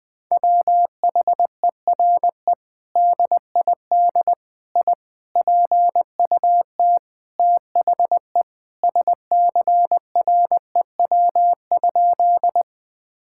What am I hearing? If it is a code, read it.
Morse code, heard as a keyed tone, about 20 wpm, 700 Hz. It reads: WHERE DID I PUT THE SCREW?
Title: Sentences from Top 1600 Words 20wpm